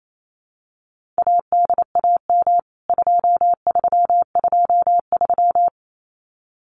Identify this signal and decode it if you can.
Morse code, heard as a keyed tone, about 28 wpm, 700 Hz. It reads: ADAM 2323
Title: CWT Exchange 28wpm